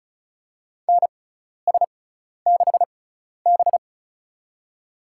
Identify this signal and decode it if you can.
Morse code, heard as a keyed tone, about 36 wpm, spaced as for 12 wpm, 700 Hz. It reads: NS6B
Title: USA Call Signs - 4 Characters - 6x Character Spacing - 35wpm